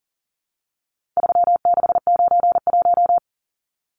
Morse code, heard as a keyed tone, about 40 wpm, 700 Hz. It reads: 3691
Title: Numbers - 4-Digit 40wpm